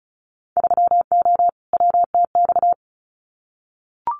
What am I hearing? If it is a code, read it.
Morse code, heard as a keyed tone, about 36 wpm, 700 Hz. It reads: 3O WTX
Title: Winter Field Day Exchanges Encore 35wpm